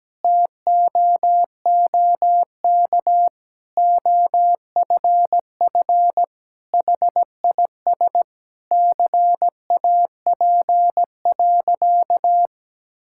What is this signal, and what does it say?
Morse code, heard as a keyed tone, about 17 wpm, 700 Hz. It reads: TOOK OFF HIS CAP.
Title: Sentences from Top 1600 Words 17wpm